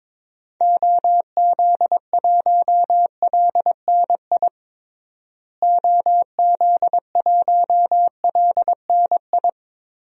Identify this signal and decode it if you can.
Morse code, heard as a keyed tone, about 22 wpm, 700 Hz. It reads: OZ1LNI OZ1LNI
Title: DXCC-100 Call Signs - 6 Characters - Repeated 3x - 22wpm